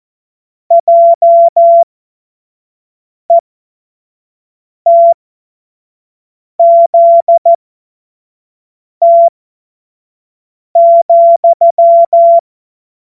Morse code, heard as a keyed tone, about 14 wpm, spaced as for 5 wpm, 680 Hz. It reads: JETZT,
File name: Morsekurs / Witze / Witz 22. November 2022 25 F70